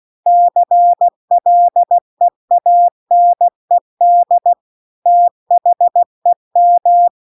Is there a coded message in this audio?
Yes: Morse code from 0.3 to 7.1 s, heard as a keyed tone, about 16 wpm, 700 Hz. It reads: CLEANED THEM